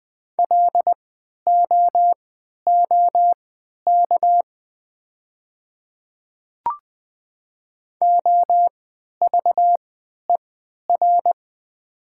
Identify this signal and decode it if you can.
Morse code, heard as a keyed tone, about 20 wpm, spaced as for 11 wpm, 700 Hz. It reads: LOOK OVER
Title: 3x Character Spacing - Top 100 Words - 20wpm